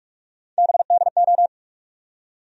Morse code, heard as a keyed tone, about 46 wpm, 700 Hz. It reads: BDO
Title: USA Call Sign Postfixes - 3 Letters 45wpm